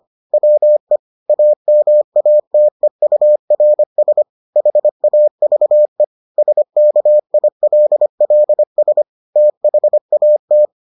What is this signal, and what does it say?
Morse code, heard as a keyed tone, about 25 wpm, 600 Hz. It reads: WE AMATEURS HAVE SKILLS THAT